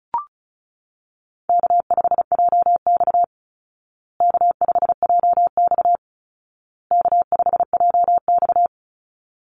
Morse code, heard as a keyed tone, about 34 wpm, 700 Hz. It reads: K5JX K5JX K5JX
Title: Call Signs Repeated 2x - 35wpm